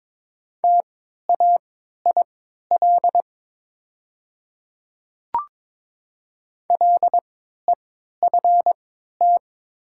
Morse code, heard as a keyed tone, about 22 wpm, spaced as for 12 wpm, 700 Hz. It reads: TAIL LEFT
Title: 3x Character Spacing - Top 500 Words - 22wpm